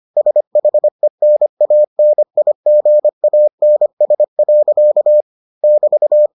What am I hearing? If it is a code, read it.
Morse code, heard as a keyed tone, about 25 wpm, 600 Hz. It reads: SHENANIGANS. <BT>